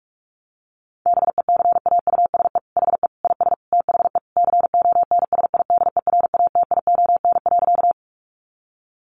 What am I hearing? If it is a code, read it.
Morse code, heard as a keyed tone, about 44 wpm, 700 Hz. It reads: BECAUSE HE IS THE CONSIDERATION.
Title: Sentences from Top 1500 Words 45wpm